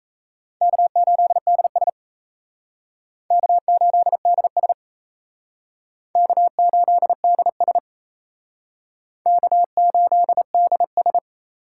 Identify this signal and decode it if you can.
Morse code, heard as a keyed tone, about 34 wpm, 700 Hz. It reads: K8DS K8DS K8DS K8DS
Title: Speed Racing - USA 1x2 Call Signs 28wpm